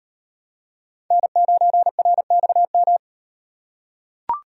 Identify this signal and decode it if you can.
Morse code, heard as a keyed tone, about 38 wpm, 700 Hz. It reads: N9RXM